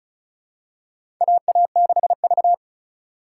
Morse code, heard as a keyed tone, about 36 wpm, 700 Hz. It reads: AA6V
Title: USA Call Signs - 4 Characters - 3x Character Spacing - 35wpm